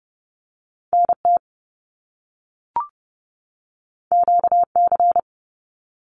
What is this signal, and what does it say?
Morse code, heard as a keyed tone, about 30 wpm, 700 Hz. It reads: NT QC
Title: Canadian Provinces and Territories 30wpm